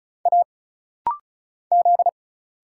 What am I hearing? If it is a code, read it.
Morse code, heard as a keyed tone, about 34 wpm, 700 Hz. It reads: A Z
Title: Single Letters - Rapid-Fire 35wpm